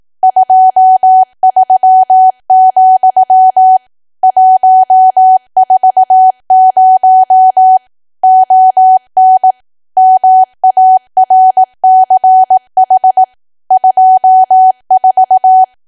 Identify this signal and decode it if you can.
Morse code, heard as a keyed tone, about 18 wpm, 750 Hz. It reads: 23, 140 ON MARCH 24
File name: cwa / W1AW-Level2 / 18WPM